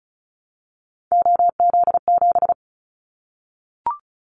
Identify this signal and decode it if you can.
Morse code, heard as a keyed tone, about 36 wpm, 700 Hz. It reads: OZ7
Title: Callsign Prefixes for Top 100 DXCC - 35wpm